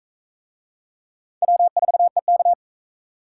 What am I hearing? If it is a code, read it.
Morse code, heard as a keyed tone, about 42 wpm, 700 Hz. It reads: W4EK